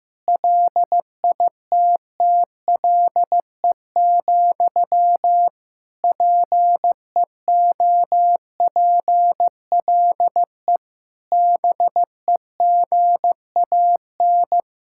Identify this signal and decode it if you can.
Morse code, heard as a keyed tone, about 15 wpm, 700 Hz. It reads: LITTLE, PEOPLE BEGAN
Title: Sentences from Top 1600 Words 15wpm